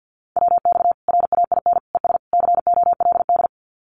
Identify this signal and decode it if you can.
Morse code, heard as a keyed tone, about 50 wpm, 700 Hz. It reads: WX RAIN ES COLD
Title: Common QSO Phrases 50wpm